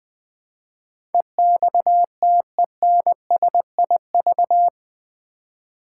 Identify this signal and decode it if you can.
Morse code, heard as a keyed tone, about 20 wpm, 700 Hz. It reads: EXTENSIV
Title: Build A Word - 4 Letters At A Time - 11-Letter Words - 20wpm